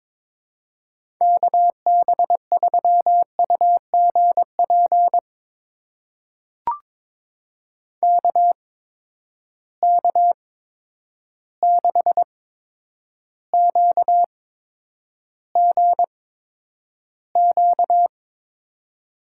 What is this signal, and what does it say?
Morse code, heard as a keyed tone, about 22 wpm, spaced as for 9 wpm, 700 Hz. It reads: KB3UGP KK6QGQ
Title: USA Call Signs - 6 Characters - 8x Character Spacing - 22wpm